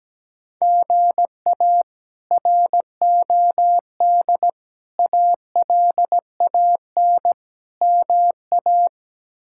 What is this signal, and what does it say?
Morse code, heard as a keyed tone, about 17 wpm, 700 Hz. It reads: GA ROD ALAN MA
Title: K1USN SST QSO Practice - 17wpm